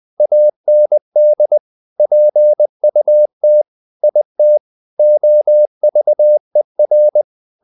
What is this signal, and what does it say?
Morse code, heard as a keyed tone, about 20 wpm, 600 Hz. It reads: AND PUT IT OVER
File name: the old man and the sea part 2 20wpm